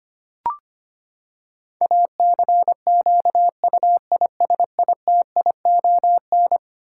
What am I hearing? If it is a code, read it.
Morse code, heard as a keyed tone, about 25 wpm, 700 Hz. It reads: ACQUISITION
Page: morse-code-ninja-practice-25wpm